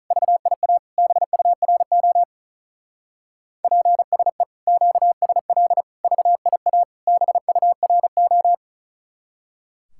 Morse code, heard as a keyed tone, about 33 wpm, 700 Hz. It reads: VIA BURO PSE QSL VIA BURO
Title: Speed-Racing Common QSO Phrases - 35wpm